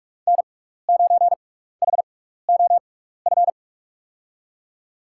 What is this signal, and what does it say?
Morse code, heard as a keyed tone, about 46 wpm, spaced as for 16 wpm, 700 Hz. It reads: N9HOF
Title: USA Call Signs - 6x Character Spacing - 45wpm